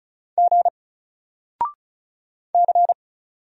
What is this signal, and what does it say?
Morse code, heard as a keyed tone, about 36 wpm, 700 Hz. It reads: G C
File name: Single-Letter-Number-35wpm